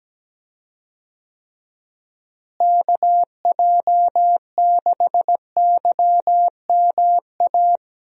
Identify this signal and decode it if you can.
Morse code, heard as a keyed tone, about 17 wpm, 700 Hz. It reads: KJ6YMA
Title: USA Call Signs - 6 Characters - Repeated 3x - 17wpm